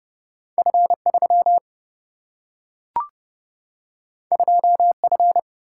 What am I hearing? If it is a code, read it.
Morse code, heard as a keyed tone, about 30 wpm, 700 Hz. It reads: F3 2F
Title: Single-Letter Single-Number 30wpm